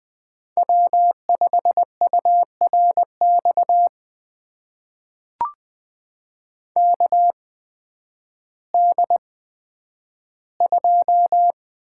Morse code, heard as a keyed tone, about 20 wpm, spaced as for 11 wpm, 700 Hz. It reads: W5URX KD2
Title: USA Call Signs - 8x Character Spacing - 20wpm